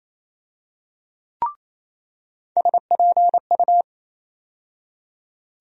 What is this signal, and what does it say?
Morse code, heard as a keyed tone, about 28 wpm, 700 Hz. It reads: SPU
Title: USA Call Sign Postfixes - 3 Letters 28wpm